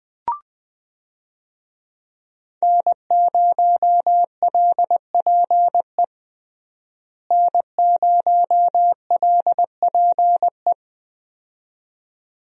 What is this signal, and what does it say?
Morse code, heard as a keyed tone, about 20 wpm, 700 Hz. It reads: N0LPE N0LPE
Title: USA Call Signs - 5 Characters - Repeated 2x - 20wpm